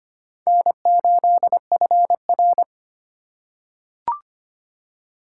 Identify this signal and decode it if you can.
Morse code, heard as a keyed tone, about 25 wpm, 700 Hz. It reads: N8FR
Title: USA Call Signs - 4 Characters - 4x Character Spacing - 25wpm